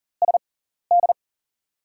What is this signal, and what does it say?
Morse code, heard as a keyed tone, about 40 wpm, spaced as for 14 wpm, 700 Hz. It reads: SD
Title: USA Call Signs - 4 Characters - 6x Character Spacing - 40wpm